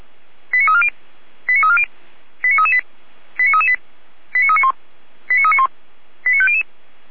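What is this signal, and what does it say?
Five-tone selective call (ZVEI), heard as quick runs of five tones, tones of about 70 ms. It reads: 89340 89340 89338 89338 89391 89391 89500
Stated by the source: ZVEI five-tone selective call 89325, received by SDR.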